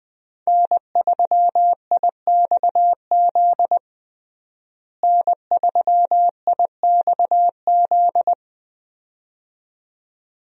Morse code, heard as a keyed tone, about 20 wpm, 700 Hz. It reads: N3IXZ N3IXZ
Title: USA Call Signs - 5 Characters - Repeated 2x - 20wpm